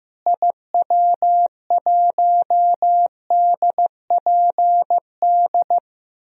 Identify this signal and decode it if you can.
Morse code, heard as a keyed tone, about 16 wpm, 700 Hz. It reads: IW1DPD
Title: DXCC-100 Call Signs - 6 Characters - Repeated 2x - 15wpm